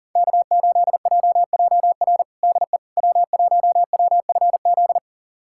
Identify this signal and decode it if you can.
Morse code, heard as a keyed tone, about 40 wpm, 700 Hz. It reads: K8JJR DE W1WFZ
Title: USA Callsign DE Callsign - 5 Characters - 40wpm